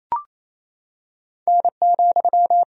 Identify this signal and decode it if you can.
Morse code, heard as a keyed tone, about 28 wpm, 700 Hz. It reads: N,
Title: Sentences from Top 200 Words and Abbreviations 28wpm